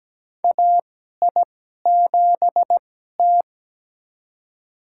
Morse code, heard as a keyed tone, about 17 wpm, spaced as for 12 wpm, 700 Hz. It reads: AI7T
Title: USA Call Signs - 4 Characters - 2x Character Spacing - 17wpm